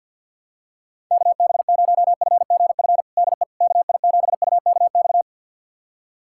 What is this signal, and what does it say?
Morse code, heard as a keyed tone, about 46 wpm, 700 Hz. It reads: KB0FGF DE KI7UKX
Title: USA Callsign DE Callsign - 6 Characters - 50wpm